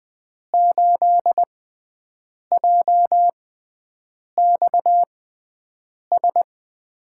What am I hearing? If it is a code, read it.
Morse code, heard as a keyed tone, about 20 wpm, spaced as for 7 wpm, 700 Hz. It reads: 8JXS
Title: USA Call Signs - 6 Characters - 6x Character Spacing - 20wpm